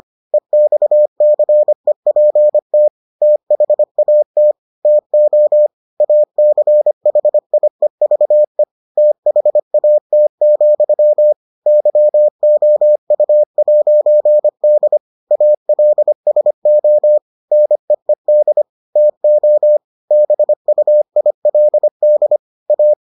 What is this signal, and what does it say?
Morse code, heard as a keyed tone, about 25 wpm, 600 Hz. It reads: EXCEPT THAT TO ACHIEVE THAT, YOU'D ALSO NEED TO BUILD A